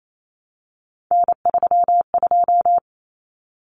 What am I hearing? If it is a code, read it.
Morse code, heard as a keyed tone, about 28 wpm, 700 Hz. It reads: N32
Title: Common Cut Numbers - 3-Digit 28wpm